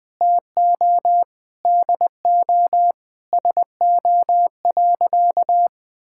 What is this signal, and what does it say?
Morse code, heard as a keyed tone, about 21 wpm, 700 Hz. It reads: TO DO SO.
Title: Sentences with 6-Word Phrases - 20wpm